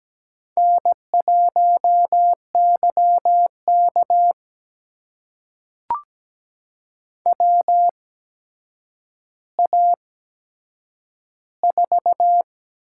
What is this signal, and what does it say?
Morse code, heard as a keyed tone, about 17 wpm, spaced as for 8 wpm, 700 Hz. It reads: N1YK WA4
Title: USA Call Signs - 8x Character Spacing - 17wpm